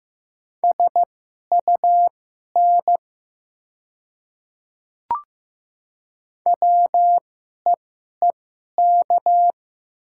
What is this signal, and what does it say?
Morse code, heard as a keyed tone, about 15 wpm, spaced as for 11 wpm, 700 Hz. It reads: SUN WEEK
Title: 2x Character Spacing - Top 500 Words - 15wpm